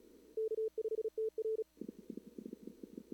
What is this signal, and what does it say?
Morse code, heard as a keyed tone, about 34 wpm, 440 Hz. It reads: K5TR
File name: CWSprint / Feb2022 / k5tr-feb-na-sprint-cw-k5ot -0000